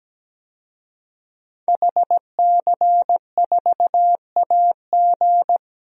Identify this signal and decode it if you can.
Morse code, heard as a keyed tone, about 17 wpm, 700 Hz. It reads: HC4AG
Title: DXCC-100 Call Signs - 5 Characters - Repeated 2x - 17wpm